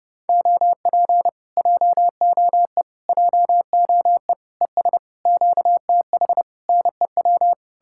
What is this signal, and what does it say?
Morse code, heard as a keyed tone, about 33 wpm, 700 Hz. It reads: OP JOE JOE ES QTH NEW